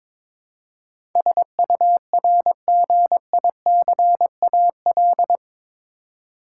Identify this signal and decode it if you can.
Morse code, heard as a keyed tone, about 22 wpm, 700 Hz. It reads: SURGICAL
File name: practice / 22wpm / Eight Letter Words - 22wpm